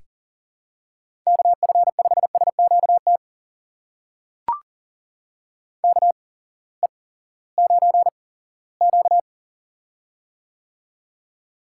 Morse code, heard as a keyed tone, about 40 wpm, spaced as for 20 wpm, 700 Hz. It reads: KF5SQT KE9Q
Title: USA Call Signs - 8x Character Spacing - 40wpm